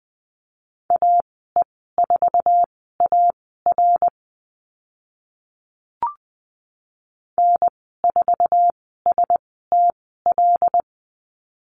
Morse code, heard as a keyed tone, about 20 wpm, spaced as for 14 wpm, 700 Hz. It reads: AE4AR N4STL
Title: 082 Lesson 14g - Number 2 - Call Signs 20wpm